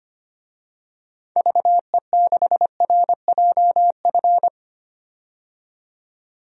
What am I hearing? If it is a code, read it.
Morse code, heard as a keyed tone, about 25 wpm, 700 Hz. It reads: VE6RJF
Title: DXCC-100 Call Signs - 6 Characters - 25wpm